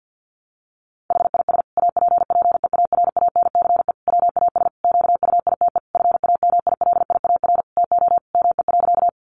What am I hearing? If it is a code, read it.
Morse code, heard as a keyed tone, about 50 wpm, 700 Hz. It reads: HIS APPEARANCE WAS QUITE FAMILIAR TO ME.
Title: Sentences from Top 1100 Words 50wpm